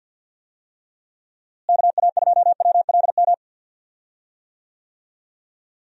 Morse code, heard as a keyed tone, about 51 wpm, 700 Hz. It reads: KA2WLM
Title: USA Call Signs - 6 Characters - Repeated 3x - 50wpm